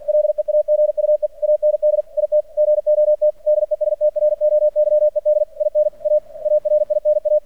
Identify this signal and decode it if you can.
Morse code, heard as a keyed tone, about 48 wpm, 600 Hz. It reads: LEAGUE AND IT GOT DESTROYED IN A ARINA
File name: archived / CRSnet